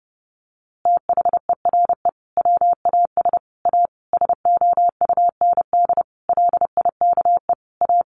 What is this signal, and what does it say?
Morse code, heard as a keyed tone, about 30 wpm, 700 Hz. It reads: THERE WAS A SOUND LIKE A
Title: A Sound Of Thunder 30wpm